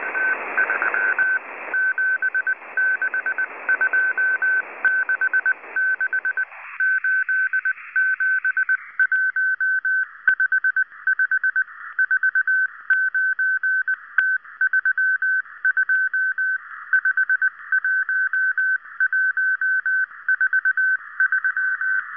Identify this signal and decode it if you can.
Morse code, heard as a keyed tone, about 19 wpm, 1550 Hz. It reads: T3 76266 87155 49T32 51143